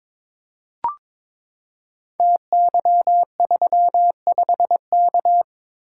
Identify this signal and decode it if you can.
Morse code, heard as a keyed tone, about 22 wpm, 700 Hz. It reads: TY35K
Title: Code Groups - 5 Letter:Numbers - 22wpm